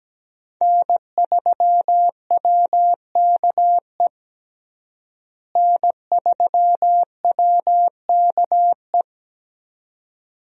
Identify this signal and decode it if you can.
Morse code, heard as a keyed tone, about 17 wpm, 700 Hz. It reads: N3WKE N3WKE
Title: USA Call Signs - 5 Characters - Repeated 2x - 17wpm